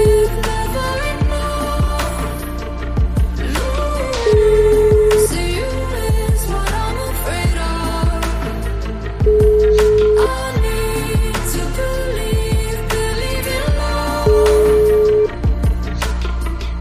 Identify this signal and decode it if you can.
Ringback tone (Europe, 425 Hz), with noise.